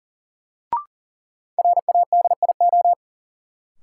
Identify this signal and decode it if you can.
Morse code, heard as a keyed tone, about 40 wpm, 700 Hz. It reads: RADIO
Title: Sentences from Top 1000 Words - Review Part 3 - New Word Drill - 40wpm